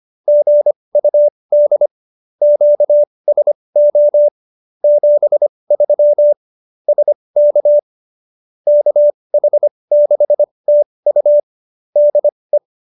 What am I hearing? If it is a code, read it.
Morse code, heard as a keyed tone, about 25 wpm, spaced as for 20 wpm, 590 Hz. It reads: GUD QSO 73 SK KH6TU DE